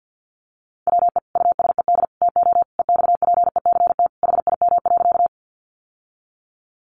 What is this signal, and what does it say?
Morse code, heard as a keyed tone, about 50 wpm, 700 Hz. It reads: WE USED TO EXPECT HIM.
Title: Sentences from Top 1100 Words 50wpm